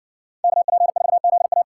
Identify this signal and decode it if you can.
Morse code, heard as a keyed tone, about 60 wpm, 700 Hz. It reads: KP4ZA